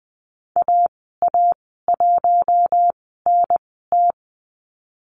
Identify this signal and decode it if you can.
Morse code, heard as a keyed tone, about 20 wpm, spaced as for 15 wpm, 700 Hz. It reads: AA1NT
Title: 048 Lesson 9g - Number 4 - Call Signs 20wpm